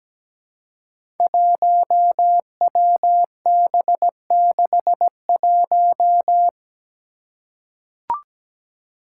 Morse code, heard as a keyed tone, about 17 wpm, 700 Hz. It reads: 1WB61
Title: Five Challenging Letter-Number Combinations 17wpm